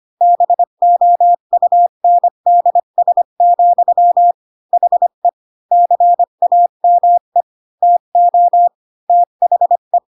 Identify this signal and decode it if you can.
Morse code, heard as a keyed tone, about 25 wpm, 700 Hz. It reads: BOUNDS, HE CAME TO THE